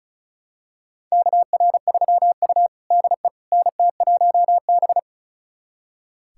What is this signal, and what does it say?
Morse code, heard as a keyed tone, about 36 wpm, 700 Hz. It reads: KR3U DE NT1B
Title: USA Callsign DE Callsign - 4 Characters - 35wpm